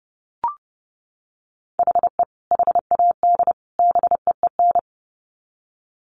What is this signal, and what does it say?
Morse code, heard as a keyed tone, about 30 wpm, 700 Hz. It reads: HE HAD BEEN
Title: Sets of 3 Words 30wpm